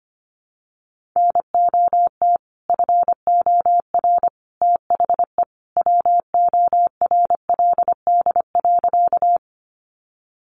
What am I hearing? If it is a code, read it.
Morse code, heard as a keyed tone, about 25 wpm, 700 Hz. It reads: NOT FOR THE WORLD.
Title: Sentences from Top 600 Words 25wpm